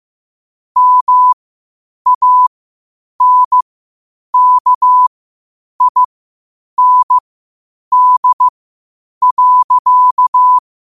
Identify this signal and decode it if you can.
Morse code, heard as a keyed tone, about 15 wpm, spaced as for 8 wpm, 1000 Hz. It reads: MANKIND.
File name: qotdfiles / 8wpm